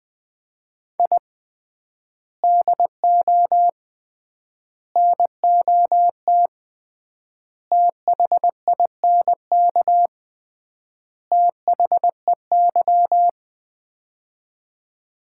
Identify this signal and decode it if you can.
Morse code, heard as a keyed tone, about 20 wpm, 700 Hz. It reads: I DO NOT THINK THEY
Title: 3x Word-Spacing - Sets of 5 Words - 20wpm